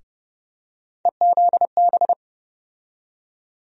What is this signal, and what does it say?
Morse code, heard as a keyed tone, about 32 wpm, 700 Hz. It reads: EZB
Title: USA Call Sign Postfixes - 3 Letters 30wpm